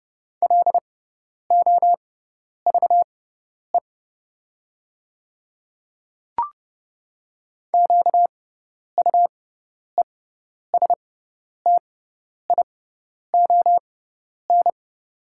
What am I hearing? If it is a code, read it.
Morse code, heard as a keyed tone, about 30 wpm, spaced as for 10 wpm, 700 Hz. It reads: LOVE QUESTION
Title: 6x Character Spacing - Top 500 Words - 30wpm